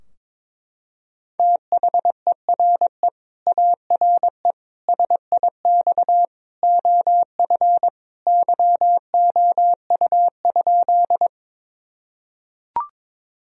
Morse code, heard as a keyed tone, about 22 wpm, 700 Hz. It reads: THERE ARE SIX OF YOU?
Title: Sentences from Top 1000 Words - Review Part 2 - 22wpm